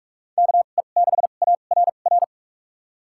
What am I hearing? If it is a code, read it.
Morse code, heard as a keyed tone, about 46 wpm, spaced as for 33 wpm, 700 Hz. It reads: KE6ARR
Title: USA Call Signs - 6 Characters - 2x Character Spacing - 45wpm